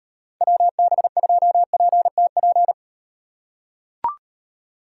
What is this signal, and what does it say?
Morse code, heard as a keyed tone, about 36 wpm, 700 Hz. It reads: WB2PTP